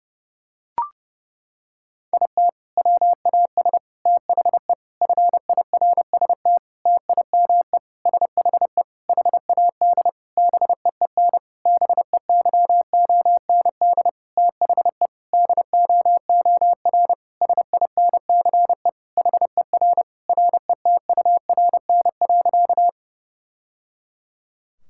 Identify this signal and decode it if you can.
Morse code, heard as a keyed tone, about 32 wpm, 700 Hz. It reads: IT WAS THE FIRST TIME SHE HAD BEEN BEYOND THE DOOR SINCE HER RETURN.
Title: Sentences from Top 1000 Words - Review Part 6 - 30wpm